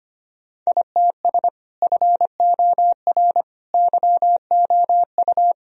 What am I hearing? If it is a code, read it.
Morse code, heard as a keyed tone, about 25 wpm, 700 Hz. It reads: ITS FOR YOU
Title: Sentences from Top 1000 Words - Review Part 3 - 25wpm